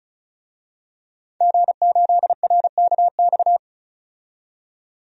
Morse code, heard as a keyed tone, about 36 wpm, 700 Hz. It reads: G8RKX
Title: DXCC-100 Call Signs - 5 Characters - 35wpm